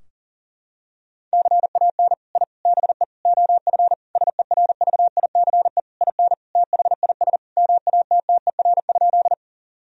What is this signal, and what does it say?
Morse code, heard as a keyed tone, about 40 wpm, 700 Hz. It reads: CAN I BE OF SERVICE IN THIS MATTER?
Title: Sentences from Top 1000 Words - Review Part 4 - 40wpm